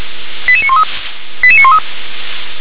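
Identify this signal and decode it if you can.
Five-tone selective call (ZVEI), heard as quick runs of five tones, tones of about 70 ms. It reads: 88913 88913
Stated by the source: ZVEI five-tone selective call received on SDR.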